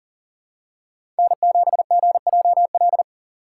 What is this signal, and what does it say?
Morse code, heard as a keyed tone, about 40 wpm, 700 Hz. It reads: N7GJL
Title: USA Call Signs - 5 Characters - 2x Character Spacing - 40wpm